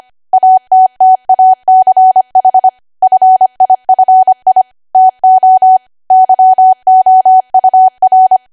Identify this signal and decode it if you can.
Morse code, heard as a keyed tone, about 25 wpm, 750 Hz. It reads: ATTACH FIFI TO YOUR
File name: cwa / W1AW-Level3 / 25WPM